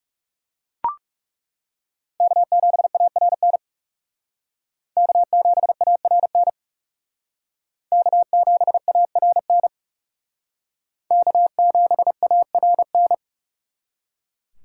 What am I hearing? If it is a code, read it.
Morse code, heard as a keyed tone, about 39 wpm, 700 Hz. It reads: K7ARN K7ARN K7ARN K7ARN